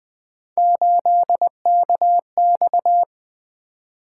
Morse code, heard as a keyed tone, about 20 wpm, 700 Hz. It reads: 8KX
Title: Three Letter-Number Combinations - 20wpm